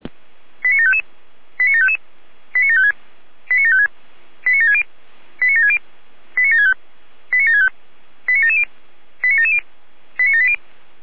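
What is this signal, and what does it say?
Five-tone selective call (ZVEI), heard as quick runs of five tones, tones of about 70 ms. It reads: 89755 89755 89756 89756 89760 89760 89765 89765 89770 89770 89780
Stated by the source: ZVEI five-tone selective call received on SDR.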